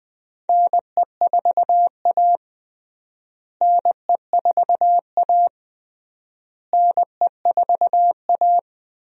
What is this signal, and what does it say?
Morse code, heard as a keyed tone, about 20 wpm, 700 Hz. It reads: NE4A NE4A NE4A
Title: USA Call Signs - 4 Characters - Repeated 3x - 20wpm